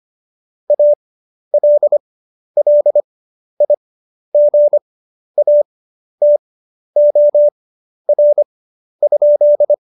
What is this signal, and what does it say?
Morse code, heard as a keyed tone, about 25 wpm, spaced as for 11 wpm, 590 Hz. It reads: ALLIGATOR?
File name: Jokes-25-char-10-eff